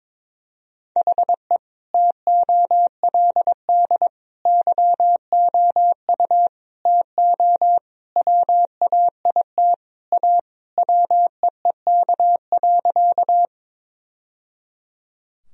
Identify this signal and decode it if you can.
Morse code, heard as a keyed tone, about 22 wpm, 700 Hz. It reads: HE TOLD YOU TO WAIT A WEEK.
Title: Sentences from Top 1000 Words - Review Part 2 - 22wpm